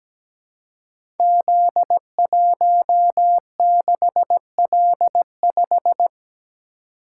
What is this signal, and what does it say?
Morse code, heard as a keyed tone, about 17 wpm, 700 Hz. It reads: Z16L5
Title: Code Groups - 5 Letter:Numbers - 17wpm